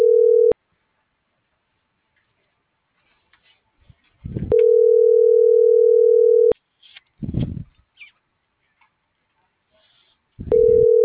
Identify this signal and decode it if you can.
Ringback tone (North America), 440 + 480 Hz.
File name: CENTRAL GOMSA / out francisco.cobos